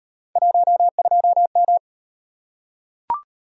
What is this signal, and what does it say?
Morse code, heard as a keyed tone, about 38 wpm, 700 Hz. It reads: 12M